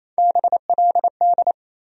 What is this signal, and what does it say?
Morse code, heard as a keyed tone, about 28 wpm, 700 Hz. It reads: BLD
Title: USA Call Sign Postfixes - 3 Letters 28wpm